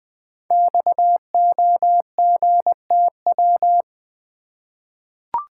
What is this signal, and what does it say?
Morse code, heard as a keyed tone, about 20 wpm, 700 Hz. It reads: XOGTW
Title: Code Groups - 5 Letter:Numbers - 20wpm